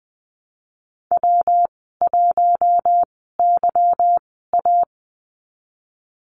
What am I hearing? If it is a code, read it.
Morse code, heard as a keyed tone, about 20 wpm, spaced as for 15 wpm, 700 Hz. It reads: W1YA
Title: 186 Lesson 30g - Letter b - Call Signs 20wpm